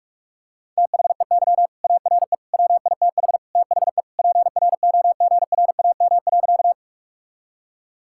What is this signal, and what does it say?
Morse code, heard as a keyed tone, about 46 wpm, 700 Hz. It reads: THEY ARE WITH THE PROGRAM.
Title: Sentences from Top 1000 Words - Review Part 5 - 45wpm